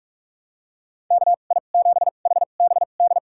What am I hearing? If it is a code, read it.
Morse code, heard as a keyed tone, about 46 wpm, spaced as for 32 wpm, 700 Hz. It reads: KI7HBD
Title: USA Call Signs - 2x Character Spacing - 45wpm